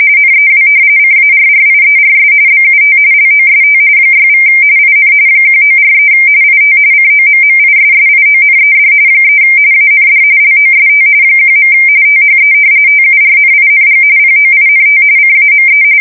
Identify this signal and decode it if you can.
RTTY, about 45.45 baud. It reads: NEW CARRIERS BUT LOST 6= IN 1943 IT COMMISSIONED 3 AND LOST 1. THE TURNING POINT CAME
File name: fsk170 / usnavy4